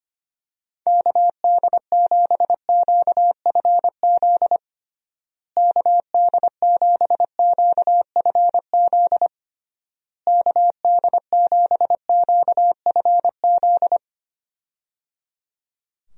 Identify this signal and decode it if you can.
Morse code, heard as a keyed tone, about 25 wpm, 700 Hz. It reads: KD7QFZ KD7QFZ KD7QFZ
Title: USA Call Signs - 6 Characters - Repeated 3x - 25wpm